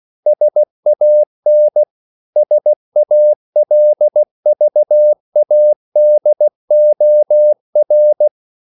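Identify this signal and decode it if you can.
Morse code, heard as a keyed tone, about 16 wpm, 600 Hz. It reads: SAN SALVADOR
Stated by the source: CW practice of national capitals at 16 words per minute, from ON6ZQ.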